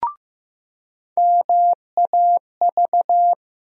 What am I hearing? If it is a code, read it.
Morse code, heard as a keyed tone, about 15 wpm, 700 Hz. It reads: MAV
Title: USA Call Sign Postfixes - 3 Letters 15wpm